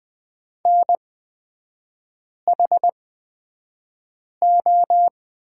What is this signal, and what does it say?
Morse code, heard as a keyed tone, about 20 wpm, spaced as for 5 wpm, 700 Hz. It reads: NHO
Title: 224 Lesson 36e - BT - Three Characters 20wpm